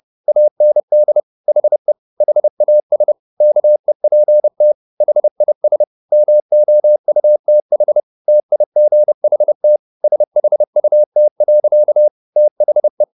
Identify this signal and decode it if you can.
Morse code, heard as a keyed tone, about 30 wpm, 600 Hz. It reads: AND HE HAS KEPT HIS MOUTH TIGHT SHUT. THE